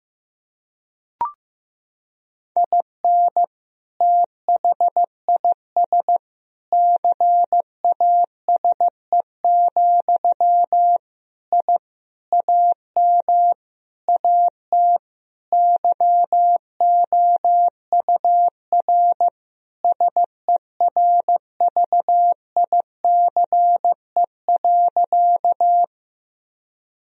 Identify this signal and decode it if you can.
Morse code, heard as a keyed tone, about 15 wpm, 700 Hz. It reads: IN THIS CASE, I AM AT YOUR SERVICE.
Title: Sentences from Top 1000 Words - Review Part 4 - 15wpm